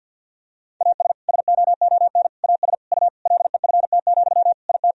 Morse code, heard as a keyed tone, about 52 wpm, 700 Hz. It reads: AS SOON AS U LEFT, IT
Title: Sentences from Top 400 Words and Abbreviations 50wpm